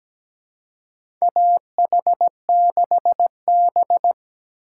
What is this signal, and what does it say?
Morse code, heard as a keyed tone, about 17 wpm, 700 Hz. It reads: AH6B
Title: USA Call Signs - 4 Characters - 4x Character Spacing - 17wpm